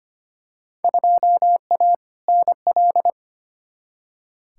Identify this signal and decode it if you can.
Morse code, heard as a keyed tone, about 25 wpm, 700 Hz. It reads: 2A NL
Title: Field Day Exchanges Encore - 25wpm